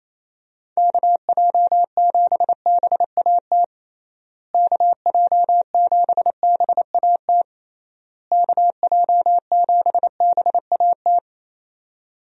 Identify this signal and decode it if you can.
Morse code, heard as a keyed tone, about 28 wpm, 700 Hz. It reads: KJ7BAT KJ7BAT KJ7BAT
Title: USA Call Signs - 6 Characters - Repeated 3x - 28wpm